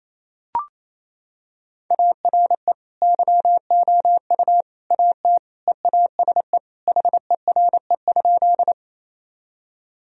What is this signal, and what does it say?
Morse code, heard as a keyed tone, about 28 wpm, 700 Hz. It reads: ARE YOU AT EASE HERE?
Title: Sentences from Top 1000 Words - Review Part 1 - 28wpm